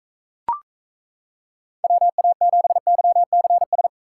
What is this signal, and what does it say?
Morse code, heard as a keyed tone, about 37 wpm, 700 Hz. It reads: WA7YCS